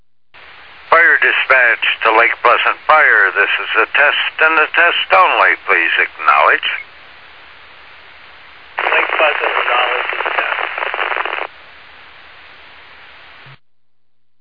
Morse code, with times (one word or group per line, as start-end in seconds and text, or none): none